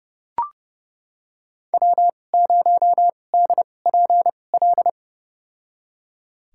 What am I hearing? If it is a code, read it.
Morse code, heard as a keyed tone, about 30 wpm, spaced as for 22 wpm, 700 Hz. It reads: W0DPL
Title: USA Call Signs - 5 Characters - 2x Character Spacing - 30wpm